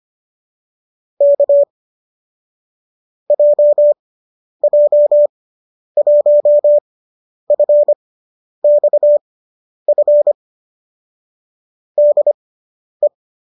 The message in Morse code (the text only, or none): K JJ1FXF DE